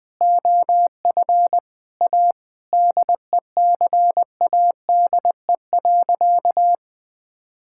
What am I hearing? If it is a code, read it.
Morse code, heard as a keyed tone, about 21 wpm, 700 Hz. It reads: OF A DECADE.
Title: Sentences with 7-Word Phrases - 20wpm